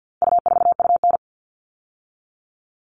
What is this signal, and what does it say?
Morse code, heard as a keyed tone, about 50 wpm, 700 Hz. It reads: U4UN
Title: All Cut Numbers - 4-Digit 50wpm